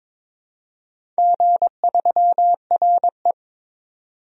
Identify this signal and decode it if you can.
Morse code, heard as a keyed tone, about 22 wpm, 700 Hz. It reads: G3RE
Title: DXCC-100 Call Signs - 4 Characters - 22wpm